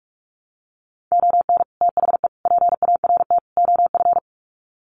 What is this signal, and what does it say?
Morse code, heard as a keyed tone, about 44 wpm, 700 Hz. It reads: ON THE PART OF
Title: Sets of 4 Words 45wpm